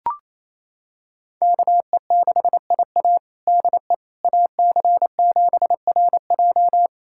Morse code, heard as a keyed tone, about 28 wpm, 700 Hz. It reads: KE6IA DE AC7RJ
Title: USA Callsign DE Callsign - 5 Characters - 28wpm